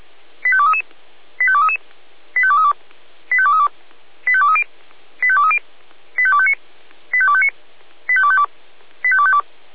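Five-tone selective call (ZVEI), heard as quick runs of five tones, tones of about 70 ms. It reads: 85233 85233 85232 85232 85230 85230 85259 85259 85252 85252
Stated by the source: ZVEI five-tone selective call 85885 received on SDR.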